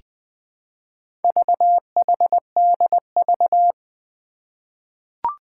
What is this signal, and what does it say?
Morse code, heard as a keyed tone, about 20 wpm, 700 Hz. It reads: VHDV
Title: Four Challenging Letter-Number Combinations 20wpm